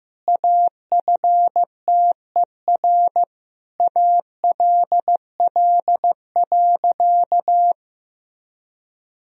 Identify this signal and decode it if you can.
Morse code, heard as a keyed tone, about 15 wpm, 700 Hz. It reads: AFTER ALL.
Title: Sentences from Top 1000 Words - Review Part 2 - 15wpm